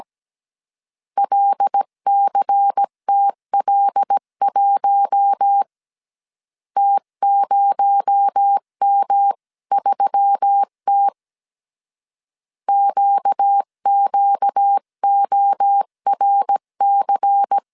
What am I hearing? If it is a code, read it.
Morse code, heard as a keyed tone, about 17 wpm, 800 Hz. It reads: LCTL1 T0M3T QQORC
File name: CW / DET / 1 6 1500C S C 16wpm 2733